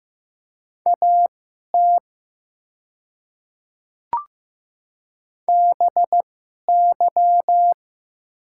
Morse code, heard as a keyed tone, about 15 wpm, spaced as for 11 wpm, 700 Hz. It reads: AT BY
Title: 2x Character Spacing - Top 100 Words - 15wpm